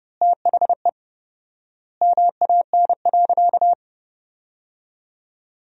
Morse code, heard as a keyed tone, about 30 wpm, 700 Hz. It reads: THE MAN.
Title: 4x Word-Spacing - Sentences from Top 400 Words - 30wpm